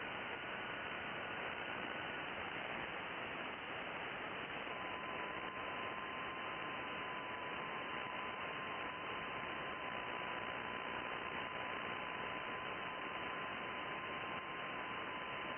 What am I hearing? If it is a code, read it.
FT8.
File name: contests / sept-vhf-k5tr- -2000